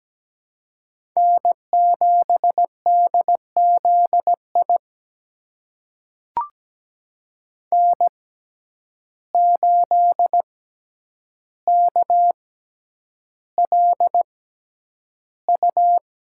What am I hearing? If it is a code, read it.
Morse code, heard as a keyed tone, about 17 wpm, spaced as for 9 wpm, 700 Hz. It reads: N7DZI N8KLU
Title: USA Call Signs - 5 Characters - 6x Character Spacing - 17wpm